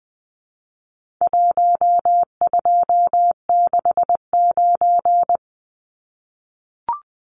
Morse code, heard as a keyed tone, about 20 wpm, 700 Hz. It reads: 1269
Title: Numbers - Contest Style - 4-Digit 20wpm